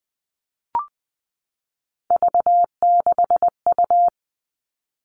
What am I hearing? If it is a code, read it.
Morse code, heard as a keyed tone, about 20 wpm, 700 Hz. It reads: V6U
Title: Three Challenging Letter-Number Combinations 20wpm